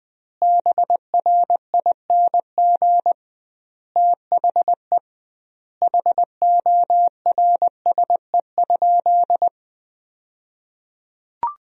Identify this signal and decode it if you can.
Morse code, heard as a keyed tone, about 21 wpm, 700 Hz. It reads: BRING THE HORSE?
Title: 2x Word-Spacing - Sentences from Top 500 Words - 20wpm